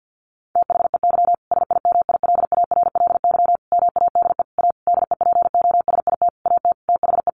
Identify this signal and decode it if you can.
Morse code, heard as a keyed tone, about 50 wpm, 700 Hz. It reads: THEY SIMILARLY MADE A DEPOSIT AT THE
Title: Sentences from Top 1400 Words 50wpm